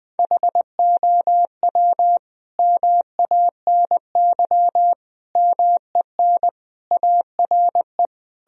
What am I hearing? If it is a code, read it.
Morse code, heard as a keyed tone, about 20 wpm, 700 Hz. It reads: HOW MANY MEN ARE
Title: Sentences from Top 1000 Words - Review Part 3 - 20wpm